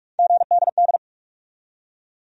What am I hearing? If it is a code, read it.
Morse code, heard as a keyed tone, about 46 wpm, 700 Hz. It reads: GDD
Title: USA Call Sign Postfixes - 3 Letters 45wpm